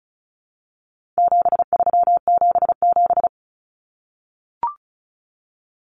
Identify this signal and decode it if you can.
Morse code, heard as a keyed tone, about 34 wpm, 700 Hz. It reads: 7377
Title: Common Cut Numbers - 4-Digit 35wpm